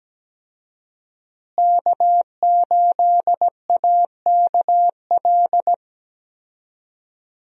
Morse code, heard as a keyed tone, about 17 wpm, 700 Hz. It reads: K8AKL
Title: USA Call Signs - 5 Characters - Repeated 2x - 17wpm